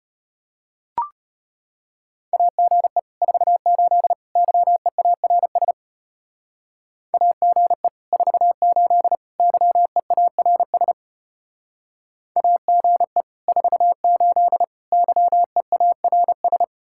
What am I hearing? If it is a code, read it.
Morse code, heard as a keyed tone, about 31 wpm, 700 Hz. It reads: AGE 48 YEARS AGE 48 YEARS AGE 48 YEARS